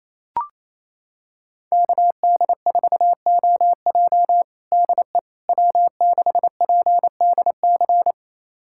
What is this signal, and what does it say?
Morse code, heard as a keyed tone, about 27 wpm, 700 Hz. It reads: KD4OJ DE W6PDC
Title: USA Callsign DE Callsign - 5 Characters - 28wpm